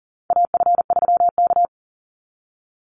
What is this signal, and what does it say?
Morse code, heard as a keyed tone, about 40 wpm, 700 Hz. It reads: AF3K
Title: Call Signs Repeated 2x - 40wpm